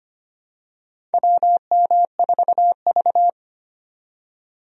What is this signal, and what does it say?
Morse code, heard as a keyed tone, about 25 wpm, 700 Hz. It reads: WM4V
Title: USA Call Signs - 4 Characters - 25wpm